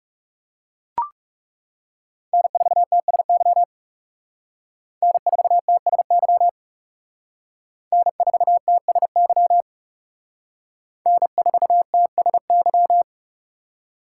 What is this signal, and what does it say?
Morse code, heard as a keyed tone, about 37 wpm, 700 Hz. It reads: N4TSY N4TSY N4TSY N4TSY
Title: Speed Racing - USA 1x3 Call Signs 30wpm